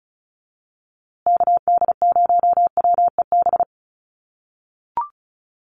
Morse code, heard as a keyed tone, about 34 wpm, 700 Hz. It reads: KD0WEB
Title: USA Call Signs Repeated 2x 35wpm